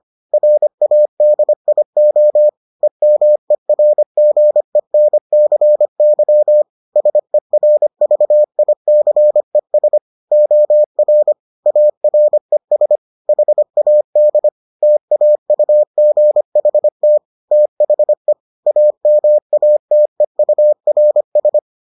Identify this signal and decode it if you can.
Morse code, heard as a keyed tone, about 25 wpm, 600 Hz. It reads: RADIO EMERGENCY SERVICES OR ARES HAD TAUGHT THE AMATEURS